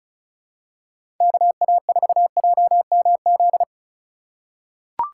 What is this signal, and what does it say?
Morse code, heard as a keyed tone, about 36 wpm, 700 Hz. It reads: KA4JMZ
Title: USA Call Signs - 6 Characters - 4x Character Spacing - 35wpm